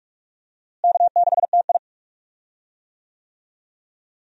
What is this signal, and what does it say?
Morse code, heard as a keyed tone, about 42 wpm, 700 Hz. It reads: K6TI
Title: USA Call Signs - 4 Characters - Repeated 2x - 45wpm